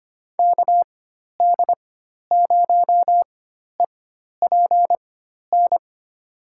Morse code, heard as a keyed tone, about 25 wpm, spaced as for 12 wpm, 700 Hz. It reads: KD0EPN
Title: USA Call Signs - 6 Characters - 4x Character Spacing - 25wpm